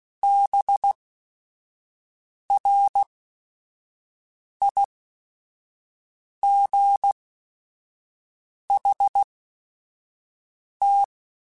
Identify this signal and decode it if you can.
Morse code, heard as a keyed tone, about 16 wpm, spaced as for 5 wpm, 800 Hz. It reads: BRIGHT
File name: Weather Sets / Weather Sets-mp3s / Weather 5 5 @5